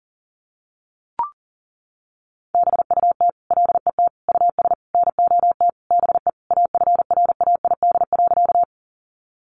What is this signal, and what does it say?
Morse code, heard as a keyed tone, about 40 wpm, 700 Hz. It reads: BUT LET US NOT BE AFRAID.
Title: Sentences from Top 1500 Words 40wpm